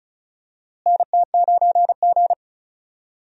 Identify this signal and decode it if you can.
Morse code, heard as a keyed tone, about 36 wpm, 700 Hz. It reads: NT9G
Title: USA Call Signs - 4 Characters - Repeated 2x - 35wpm